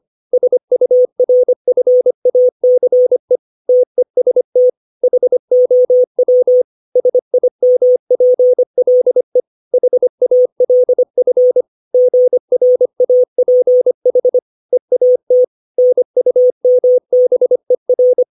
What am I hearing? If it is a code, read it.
Morse code, heard as a keyed tone, about 25 wpm, 500 Hz. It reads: SURFACE TEST HOW SIMPLE HALF GRAPH EAT NUMBER
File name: 632 words / 632 Words 25WPM